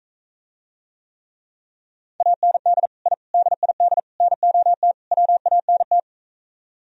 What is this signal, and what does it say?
Morse code, heard as a keyed tone, about 45 wpm, 700 Hz. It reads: AND I DID NOT WANT